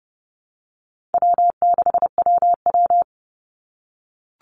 Morse code, heard as a keyed tone, about 30 wpm, 700 Hz. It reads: W6WW
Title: NA SOTA Chaser Callsigns 30wpm